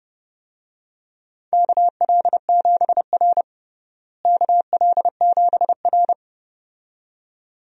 Morse code, heard as a keyed tone, about 30 wpm, 700 Hz. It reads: KL7R KL7R
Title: DXCC-100 Call Signs - 4 Characters - Repeated 2x - 30wpm